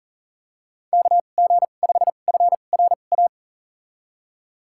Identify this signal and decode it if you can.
Morse code, heard as a keyed tone, about 40 wpm, spaced as for 29 wpm, 700 Hz. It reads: KG5FRA
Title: USA Call Signs - 2x Character Spacing - 40wpm